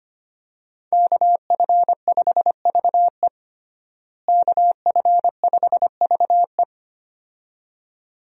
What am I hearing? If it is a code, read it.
Morse code, heard as a keyed tone, about 25 wpm, 690 Hz. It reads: KF5VE KF5VE
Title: USA Call Signs - 5 Characters - Repeated 2x - 25wpm